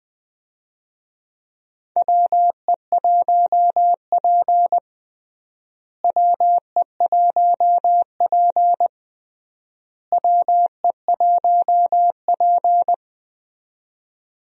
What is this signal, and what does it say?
Morse code, heard as a keyed tone, about 21 wpm, 700 Hz. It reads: WE1P WE1P WE1P
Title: USA Call Signs - 4 Characters - Repeated 3x - 20wpm